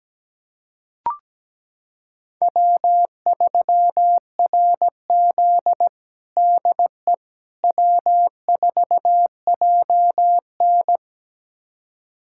Morse code, heard as a keyed tone, about 17 wpm, 700 Hz. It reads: W3RZ DE W4JN
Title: USA Callsign DE Callsign - 4 Characters - 17wpm